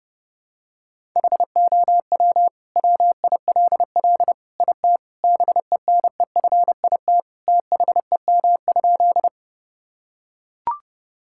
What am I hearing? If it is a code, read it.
Morse code, heard as a keyed tone, about 30 wpm, 700 Hz. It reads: HOW WILL IT BENEFIT THEM?
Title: Sentences from Top 1000 Words - Review Part 6 - 30wpm